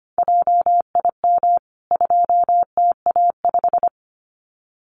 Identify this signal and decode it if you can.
Morse code, heard as a keyed tone, about 25 wpm, 700 Hz. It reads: JIM 2TA5
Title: CWT Exchange 25wpm